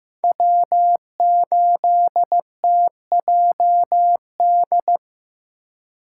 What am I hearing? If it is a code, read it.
Morse code, heard as a keyed tone, about 18 wpm, 700 Hz. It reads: W8TJD
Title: Speed Racing - USA 1x3 Call Signs 15wpm